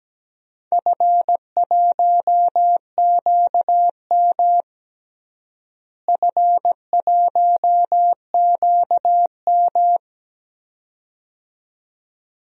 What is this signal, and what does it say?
Morse code, heard as a keyed tone, about 17 wpm, 700 Hz. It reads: F1QM F1QM
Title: DXCC-100 Call Signs - 4 Characters - Repeated 2x - 17wpm